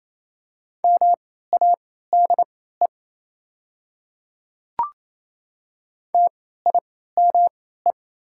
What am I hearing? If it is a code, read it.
Morse code, heard as a keyed tone, about 28 wpm, spaced as for 16 wpm, 690 Hz. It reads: MADE TIME
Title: 3x Character Spacing - Top 200 Words - 28wpm